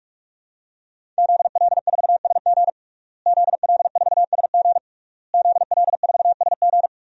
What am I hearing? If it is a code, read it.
Morse code, heard as a keyed tone, about 41 wpm, 700 Hz. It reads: ZL4SG ZL4SG ZL4SG
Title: DXCC-100 Call Signs - 5 Characters - Repeated 3x - 45wpm